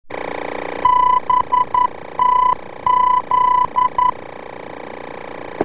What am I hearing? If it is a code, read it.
Morse code, heard as a keyed tone, about 11 wpm, 1000 Hz. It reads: BTZ
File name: BTZ-VOR-F